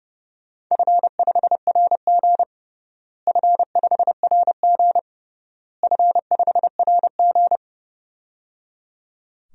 Morse code, heard as a keyed tone, about 30 wpm, 700 Hz. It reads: F5RG F5RG F5RG
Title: DXCC-100 Call Signs - 4 Characters - Repeated 3x - 30wpm